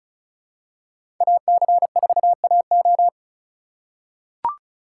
Morse code, heard as a keyed tone, about 34 wpm, 700 Hz. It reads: AC4AO
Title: USA Call Signs - 5 Characters - 3x Character Spacing - 35wpm